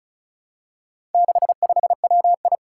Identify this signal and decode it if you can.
Morse code, heard as a keyed tone, about 36 wpm, 700 Hz. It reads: 65WI
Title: Four Challenging Letter-Number Combinations 35wpm